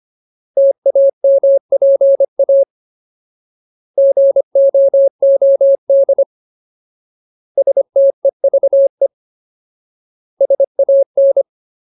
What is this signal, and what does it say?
Morse code, heard as a keyed tone, about 25 wpm, 550 Hz. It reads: TAMPA GOOD STEVE SAN
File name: Ultimate Word List / 25WPM Ultimate